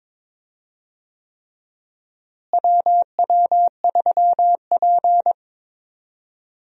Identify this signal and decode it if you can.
Morse code, heard as a keyed tone, about 21 wpm, 700 Hz. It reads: WW3P
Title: USA Call Signs - 4 Characters - Repeated 3x - 22wpm